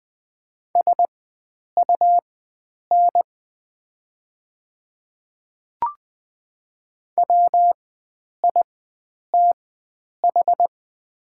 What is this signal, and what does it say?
Morse code, heard as a keyed tone, about 20 wpm, spaced as for 9 wpm, 700 Hz. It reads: SUN WITH
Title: 4x Character Spacing - Top 300 Words - 20wpm